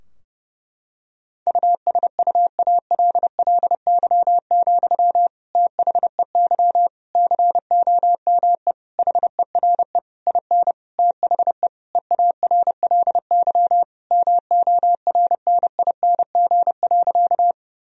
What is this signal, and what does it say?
Morse code, heard as a keyed tone, about 30 wpm, 700 Hz. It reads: USUALLY, THEY COME HERE IN THE EARLY MORNING.
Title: Sentences from Top 1000 Words - Review Part 3 - 30wpm